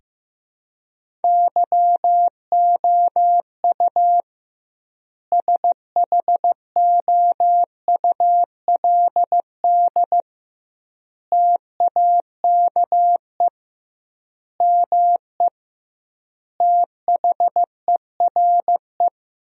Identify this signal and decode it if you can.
Morse code, heard as a keyed tone, about 15 wpm, 700 Hz. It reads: YOU SHOULD TAKE ME THERE
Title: 2x Word-Spacing - Sentences from Top 300 Words - 15wpm